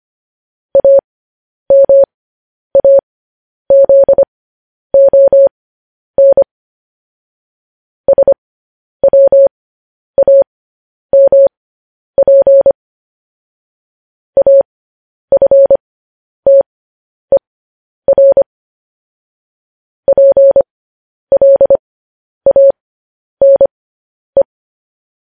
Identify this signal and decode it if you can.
Morse code, heard as a keyed tone, about 25 wpm, spaced as for 10 wpm, 550 Hz. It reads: AMAZON SWAMP AFTER PLANE
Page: news-headlines-in-morse-code-at-10-wpm